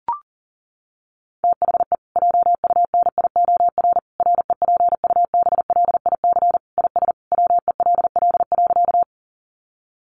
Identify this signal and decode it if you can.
Morse code, heard as a keyed tone, about 40 wpm, 700 Hz. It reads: THE JUNIOR REPUBLIC IS WELL.
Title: Sentences from Top 1500 Words 40wpm